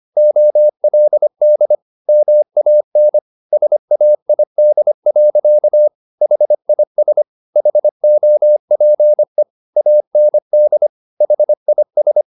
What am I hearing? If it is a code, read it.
Morse code, heard as a keyed tone, about 25 wpm, 600 Hz. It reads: OLD MAN SAID. HIS HOPE AND HIS
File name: the old man and the sea part 1 25wpm